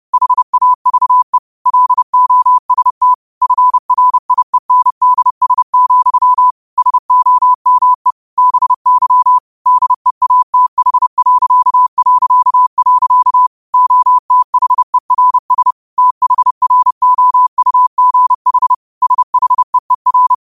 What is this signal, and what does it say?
Morse code, heard as a keyed tone, about 30 wpm, 1000 Hz. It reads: HAVE LOST FRIENDS, SOME BY DEATH... OTHERS THROUGH SHEER
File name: qotdfiles / 30wpm